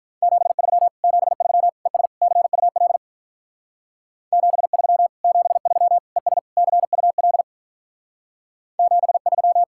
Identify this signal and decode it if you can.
Morse code, heard as a keyed tone, about 44 wpm, 700 Hz. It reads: 73 73 ES CUL 73 73 ES CUL 73